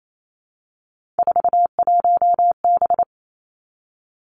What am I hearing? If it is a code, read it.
Morse code, heard as a keyed tone, about 28 wpm, 690 Hz. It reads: 41B
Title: Three Challenging Letter-Number Combinations 28wpm